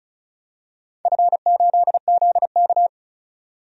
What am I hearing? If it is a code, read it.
Morse code, heard as a keyed tone, about 34 wpm, 700 Hz. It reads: F8ZK
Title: DXCC-100 Call Signs - 4 Characters - 35wpm